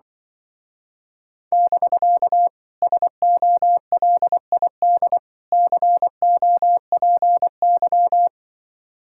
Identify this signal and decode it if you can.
Morse code, heard as a keyed tone, about 25 wpm, 700 Hz. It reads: <BK> SOLID COPY
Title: Speed-Racing Common QSO Phrases - 20wpm